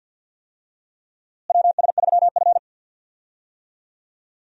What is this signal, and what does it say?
Morse code, heard as a keyed tone, about 50 wpm, 700 Hz. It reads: WS3F
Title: USA Call Signs - 4 Characters - Repeated 2x - 50wpm